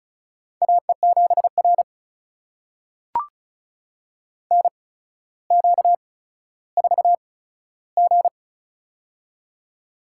Morse code, heard as a keyed tone, about 36 wpm, spaced as for 15 wpm, 700 Hz. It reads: AE7R NQ4G
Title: USA Call Signs - 4 Characters - 8x Character Spacing - 35wpm